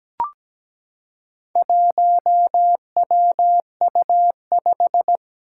Morse code, heard as a keyed tone, about 17 wpm, 700 Hz. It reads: 1WU5
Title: Four Challenging Letter-Number Combinations 17wpm